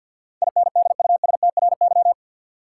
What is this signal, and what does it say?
Morse code, heard as a keyed tone, about 52 wpm, 700 Hz. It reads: INDUSTRY
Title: Sentences from Top 1300 Words - New Word Drill Alt - 50wpm